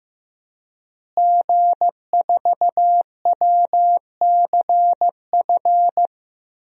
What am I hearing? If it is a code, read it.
Morse code, heard as a keyed tone, about 16 wpm, 700 Hz. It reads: G4WCF
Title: DXCC-100 Call Signs - 5 Characters - Repeated 3x - 15wpm